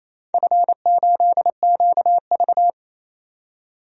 Morse code, heard as a keyed tone, about 28 wpm, 700 Hz. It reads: F8QV
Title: DXCC-100 Call Signs - 4 Characters - 28wpm